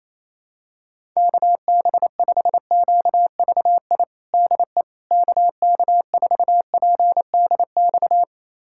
Morse code, heard as a keyed tone, about 28 wpm, 700 Hz. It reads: KB5QVI DE KK4PDX
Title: USA Callsign DE Callsign - 6 Characters - 28wpm